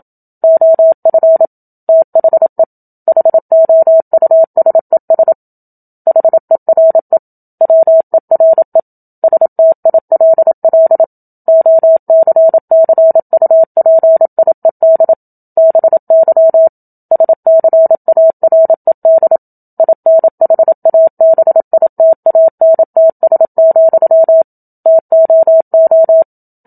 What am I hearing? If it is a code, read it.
Morse code, heard as a keyed tone, about 27 wpm, 650 Hz. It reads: OF THE HOUSES HERE WERE STILL OCCUPIED BY SCARED INHABITANTS, TOO
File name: Book Two / 01-Chapter 1 - 27 WPM